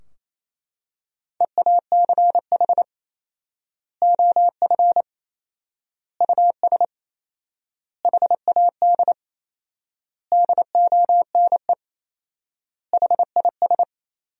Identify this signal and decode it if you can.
Morse code, heard as a keyed tone, about 27 wpm, 700 Hz. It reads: EACH OF US HAD DONE HIS
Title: 4x Word-Spacing - Sentences from Top 500 Words - 28wpm